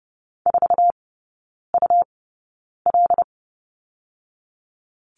Morse code, heard as a keyed tone, about 30 wpm, spaced as for 9 wpm, 700 Hz. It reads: 4UL
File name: Lesson 17e - Letter u - Three Characters 30wpm